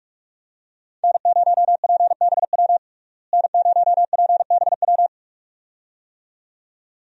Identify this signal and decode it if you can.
Morse code, heard as a keyed tone, about 42 wpm, 700 Hz. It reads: N0PBW N0PBW
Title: USA Call Signs - 5 Characters - Repeated 2x - 45wpm